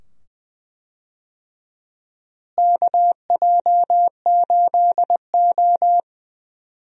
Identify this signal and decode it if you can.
Morse code, heard as a keyed tone, about 20 wpm, 700 Hz. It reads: KJ8O
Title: USA Call Signs - 4 Characters - Repeated 2x - 20wpm